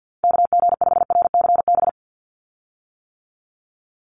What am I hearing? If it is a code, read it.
Morse code, heard as a keyed tone, about 48 wpm, 700 Hz. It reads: KG5RCB
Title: USA Call Signs 50wpm